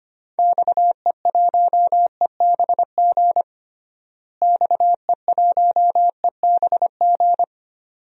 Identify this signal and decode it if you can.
Morse code, heard as a keyed tone, about 25 wpm, 700 Hz. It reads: XE1EBG XE1EBG
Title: DXCC-100 Call Signs - 6 Characters - Repeated 2x - 25wpm